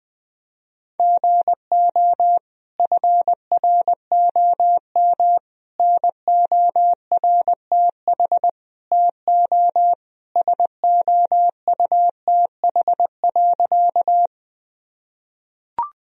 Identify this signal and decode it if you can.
Morse code, heard as a keyed tone, about 20 wpm, 700 Hz. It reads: GO FROM NORTH TO SOUTH.
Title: Sentences from Top 1000 Words - Review Part 2 - 20wpm